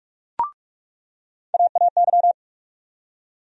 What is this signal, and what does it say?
Morse code, heard as a keyed tone, about 45 wpm, 700 Hz. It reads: AAY
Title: USA Call Sign Postfixes - 3 Letters 45wpm